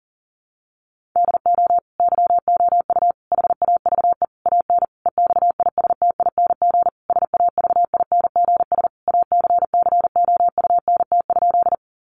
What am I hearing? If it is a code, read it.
Morse code, heard as a keyed tone, about 40 wpm, 700 Hz. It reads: DO YOU HAVE AN EXISTING SAVINGS ACCOUNT?
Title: Sentences from Top 1500 Words 40wpm